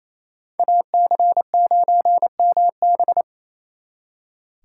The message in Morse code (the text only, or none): AC9MB